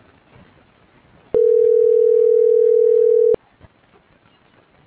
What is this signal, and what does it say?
Ringback tone (North America), 440 + 480 Hz.